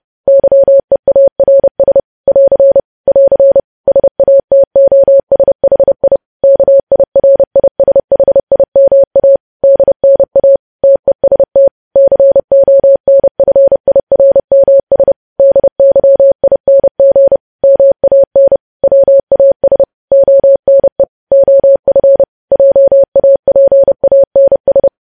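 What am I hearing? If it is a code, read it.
Morse code, heard as a keyed tone, about 30 wpm, 550 Hz. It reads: YEARS <AR> <AR> SATOSHI KIRISHIMA DNA TEST CONFIRMS DYING MAN WAS ONE OF JAPANS
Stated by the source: Morse code news headlines sent at 30, preceded by vvv vvv.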